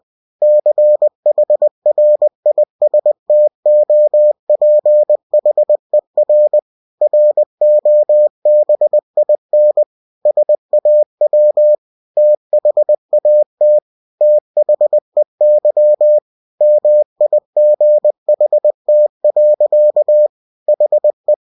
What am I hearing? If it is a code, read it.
Morse code, heard as a keyed tone, about 20 wpm, 600 Hz. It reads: CHRISTOPHER ROBIN SAW THAT THEY MIGHT. HE